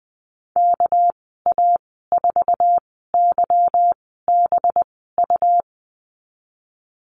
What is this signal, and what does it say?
Morse code, heard as a keyed tone, about 20 wpm, spaced as for 15 wpm, 700 Hz. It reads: KA4YBU
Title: 207 Lesson 33g - Letter j - Call Signs 20wpm